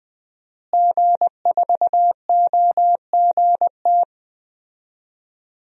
Morse code, heard as a keyed tone, about 21 wpm, 700 Hz. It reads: G4OGT
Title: DXCC-100 Call Signs - 5 Characters - Repeated 2x - 20wpm